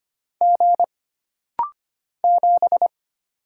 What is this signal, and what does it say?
Morse code, heard as a keyed tone, about 25 wpm, 700 Hz. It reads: G 7
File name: Single Character (Letter:Number:Punctuation:Prosign) - Rapid-Fire - 25wpm